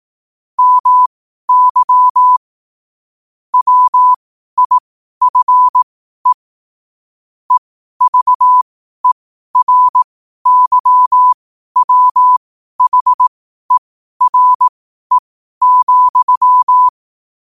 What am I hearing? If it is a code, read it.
Morse code, heard as a keyed tone, about 18 wpm, spaced as for 12 wpm, 1000 Hz. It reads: MY WIFE EVERYWHERE,